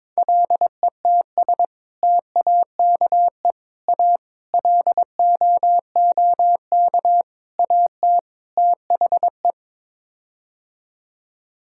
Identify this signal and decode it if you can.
Morse code, heard as a keyed tone, about 22 wpm, 700 Hz. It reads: LETS TAKE A LOOK AT THE
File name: practice / 22wpm / Sets of 6 Words - 22wpm